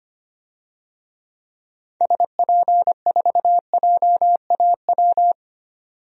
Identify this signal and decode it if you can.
Morse code, heard as a keyed tone, about 25 wpm, 700 Hz. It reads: SP4JAW
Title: DXCC-100 Call Signs - 6 Characters - 25wpm